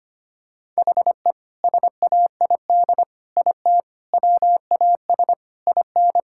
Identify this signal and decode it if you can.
Morse code, heard as a keyed tone, about 25 wpm, 700 Hz. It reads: HE SAID IT WAS IN
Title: Sentences from Top 1600 Words 25wpm